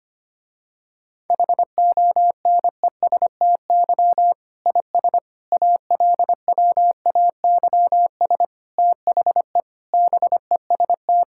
Morse code, heard as a keyed tone, about 25 wpm, 700 Hz. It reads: HONESTY IS ALWAYS THE BEST
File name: practice / 25wpm / Difficult Sentences with Hints - 25wpm